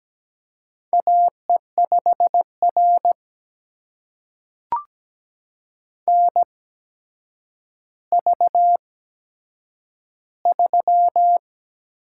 Morse code, heard as a keyed tone, about 17 wpm, spaced as for 8 wpm, 700 Hz. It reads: AE5R NV3
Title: USA Call Signs - 4 Characters - 8x Character Spacing - 17wpm